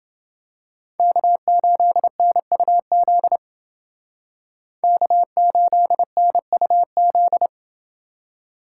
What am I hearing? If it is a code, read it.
Morse code, heard as a keyed tone, about 29 wpm, 700 Hz. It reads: K8NUZ K8NUZ